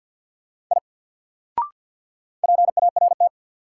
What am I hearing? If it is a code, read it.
Morse code, heard as a keyed tone, about 47 wpm, 700 Hz. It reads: I PART
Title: Sentences from Top 200 Words - New Word Drill - 50wpm